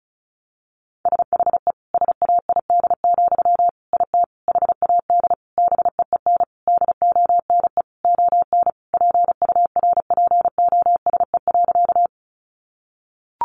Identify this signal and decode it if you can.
Morse code, heard as a keyed tone, about 34 wpm, 700 Hz. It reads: SHE SAID, IT HAD BEEN DONE ON PURPOSE.
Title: Sentences from Top 800 Words 35wpm